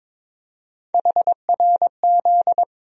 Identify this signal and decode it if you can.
Morse code, heard as a keyed tone, about 21 wpm, 700 Hz. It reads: HRZ
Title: USA Call Sign Postfixes - 3 Letters 22wpm